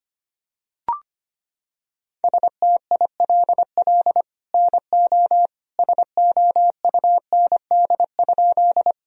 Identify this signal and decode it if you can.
Morse code, heard as a keyed tone, about 25 wpm, 700 Hz. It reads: STILL NO SOUND?
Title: Sentences from Top 1000 Words - Review Part 1 - 25wpm